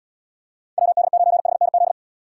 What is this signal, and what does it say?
Morse code, heard as a keyed tone, about 78 wpm, 700 Hz. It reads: WD9HNZ